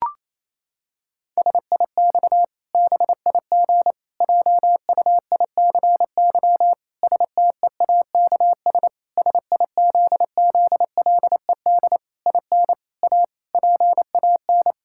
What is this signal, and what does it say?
Morse code, heard as a keyed tone, about 28 wpm, 700 Hz. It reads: SIX BIG JUICY STEAKS SIZZLED IN A PAN
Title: Pangram Challenge v2 - 28wpm